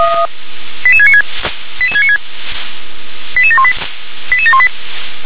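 Five-tone selective call (ZVEI), heard as quick runs of five tones, tones of about 70 ms. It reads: 88696 88696 88718 88718
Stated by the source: ZVEI five-tone call received on SDR.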